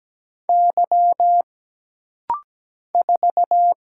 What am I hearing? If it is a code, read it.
Morse code, heard as a keyed tone, about 17 wpm, 700 Hz. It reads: Y 4
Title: Single Letter-Number Alt - Rapid-Fire 17wpm